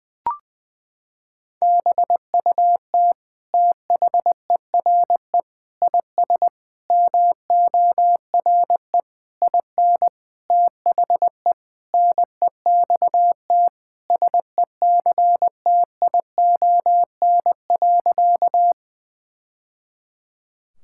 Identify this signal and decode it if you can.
Morse code, heard as a keyed tone, about 20 wpm, 700 Hz. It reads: BUT THERE IS MORE IN THE NEXT SECTION.
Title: Sentences from Top 1000 Words - Review Part 5 - 20wpm